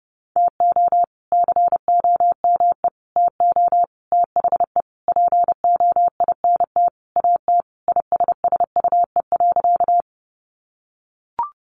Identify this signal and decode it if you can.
Morse code, heard as a keyed tone, about 30 wpm, 700 Hz. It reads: TO COME TO THE POINT AT ISSUE.
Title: Sentences from Top 900 Words 30wpm